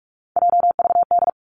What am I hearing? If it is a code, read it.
Morse code, heard as a keyed tone, about 46 wpm, 700 Hz. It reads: JVD
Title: Three Challenging Letter-Number Combinations 45wpm